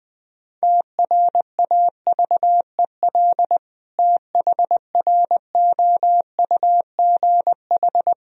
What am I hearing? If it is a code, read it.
Morse code, heard as a keyed tone, about 20 wpm, 700 Hz. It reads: TRAVEL THROUGH
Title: Sentences from Top 1000 Words - Review Part 2 - 20wpm